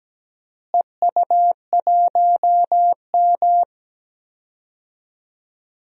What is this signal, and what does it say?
Morse code, heard as a keyed tone, about 17 wpm, 700 Hz. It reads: EU1M
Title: DXCC-100 Call Signs - 4 Characters - Repeated 2x - 17wpm